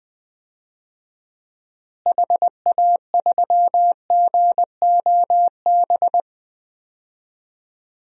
Morse code, heard as a keyed tone, about 20 wpm, 700 Hz. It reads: HA3GOB
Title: DXCC-100 Call Signs - 6 Characters - Repeated 3x - 20wpm